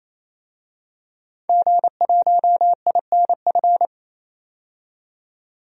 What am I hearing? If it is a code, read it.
Morse code, heard as a keyed tone, about 28 wpm, 700 Hz. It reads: G1INF
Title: DXCC-100 Call Signs - 5 Characters - 28wpm